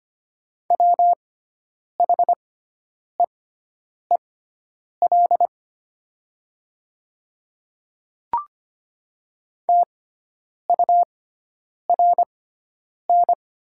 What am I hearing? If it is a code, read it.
Morse code, heard as a keyed tone, about 25 wpm, spaced as for 9 wpm, 700 Hz. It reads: WHEEL TURN
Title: 6x Character Spacing - Top 500 Words - 25wpm